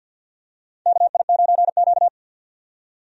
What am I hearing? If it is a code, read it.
Morse code, heard as a keyed tone, about 47 wpm, 700 Hz. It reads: KI9Q
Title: USA Call Signs - 4 Characters - 50wpm